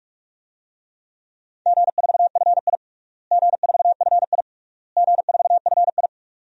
Morse code, heard as a keyed tone, about 41 wpm, 700 Hz. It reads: G4FI G4FI G4FI
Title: DXCC-100 Call Signs - 4 Characters - Repeated 3x - 45wpm